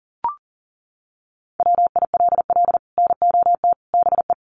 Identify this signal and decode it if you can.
Morse code, heard as a keyed tone, about 40 wpm, 700 Hz. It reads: WILL NOT BE
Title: Sets of 5 Words 40wpm